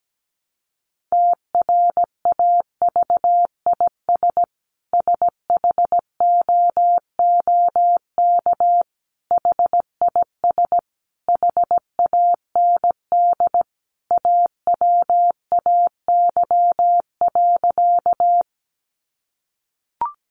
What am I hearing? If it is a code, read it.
Morse code, heard as a keyed tone, about 17 wpm, 700 Hz. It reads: TRAVIS SHOOK HIS HAND AWAY.
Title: A Sound Of Thunder 17wpm